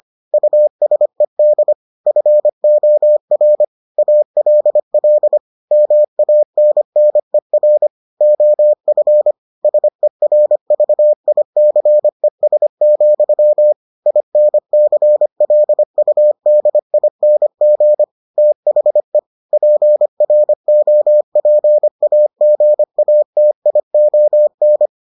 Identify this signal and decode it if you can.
Morse code, heard as a keyed tone, about 25 wpm, 600 Hz. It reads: USED FOR ALL MANNER OF SERVICES, INCLUDING THE PROPAGATION